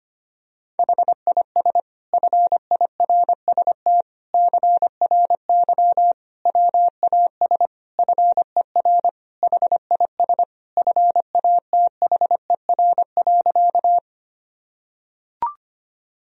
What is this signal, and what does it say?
Morse code, heard as a keyed tone, about 25 wpm, 700 Hz. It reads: HIS FIRST CRY WAS FER HIS FATHER.
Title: Sentences from Top 500 Words and Abbreviations 25wpm